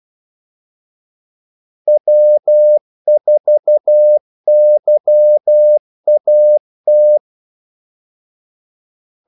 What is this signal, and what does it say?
Morse code, heard as a keyed tone, about 12 wpm, 600 Hz. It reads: W4YAT
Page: 2 ekstra ordmellemrum mellem kaldesignalerne